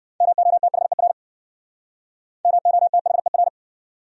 Morse code, heard as a keyed tone, about 65 wpm, 700 Hz. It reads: MOTHER MOTHER